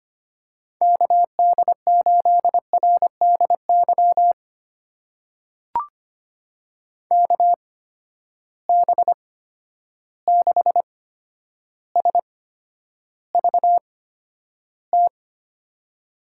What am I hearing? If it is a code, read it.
Morse code, heard as a keyed tone, about 25 wpm, spaced as for 11 wpm, 700 Hz. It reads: KD8RDY KB6SVT
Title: USA Call Signs - 6 Characters - 8x Character Spacing - 25wpm